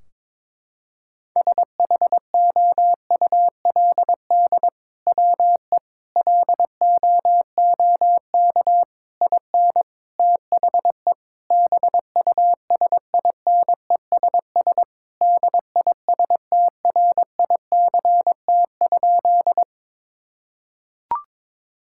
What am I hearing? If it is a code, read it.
Morse code, heard as a keyed tone, about 22 wpm, 700 Hz. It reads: SHOULD WE LOOK IN THE BUSINESS DISTRICT?
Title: Sentences from Top 1600 Words 22wpm